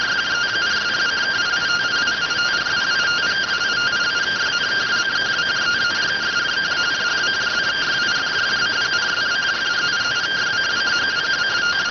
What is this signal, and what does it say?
Two-tone teleprinter signal, about 170 Hz shift.